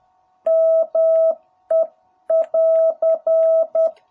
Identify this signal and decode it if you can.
Morse code, heard as a keyed tone, about 10 wpm, 660 Hz. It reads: ME <AR>
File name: jh8cbh / 2021cw